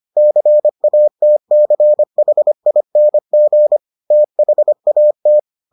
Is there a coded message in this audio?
Yes: Morse code from 0.2 to 5.4 s, heard as a keyed tone, about 25 wpm, 600 Hz. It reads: CATCHING THAT